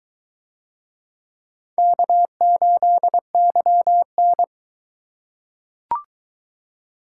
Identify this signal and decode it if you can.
Morse code, heard as a keyed tone, about 21 wpm, 700 Hz. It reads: K8YN